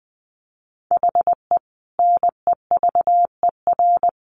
Morse code, heard as a keyed tone, about 20 wpm, 700 Hz. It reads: HE NEVER
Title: Sentences from Top 1100 Words 20wpm